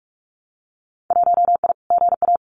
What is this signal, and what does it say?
Morse code, heard as a keyed tone, about 46 wpm, 700 Hz. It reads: 1I GA
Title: Winter Field Day Exchanges Encore 45wpm